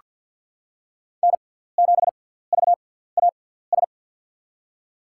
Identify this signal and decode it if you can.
Morse code, heard as a keyed tone, about 48 wpm, 700 Hz. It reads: N 7 V A S
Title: USA Call Signs - 5 Characters - 6x Character Spacing - 50wpm